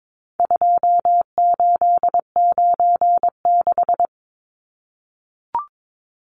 Morse code, heard as a keyed tone, about 22 wpm, 700 Hz. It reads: 2896
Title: Numbers - 4-Digit 22wpm